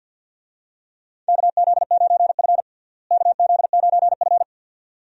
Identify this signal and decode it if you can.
Morse code, heard as a keyed tone, about 50 wpm, 700 Hz. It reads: KZ9F KZ9F
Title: USA Call Signs - 4 Characters - Repeated 2x - 50wpm